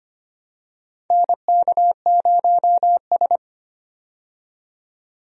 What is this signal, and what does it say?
Morse code, heard as a keyed tone, about 25 wpm, 700 Hz. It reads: NK0S
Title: USA Call Signs - 4 Characters - 25wpm